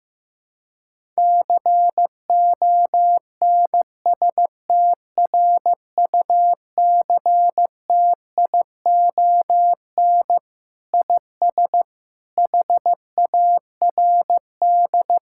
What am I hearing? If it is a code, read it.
Morse code, heard as a keyed tone, about 16 wpm, 700 Hz. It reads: CONSTRUCTION IS HARD
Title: Sentences from Top 1600 Words 15wpm